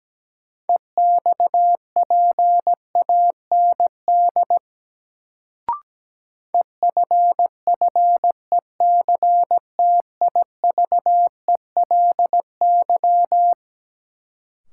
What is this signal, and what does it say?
Morse code, heard as a keyed tone, about 17 wpm, 700 Hz. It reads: EXPAND EFFECTIVELY
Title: Sentences from Top 1400 Words - New Word Drill Alt - 17wpm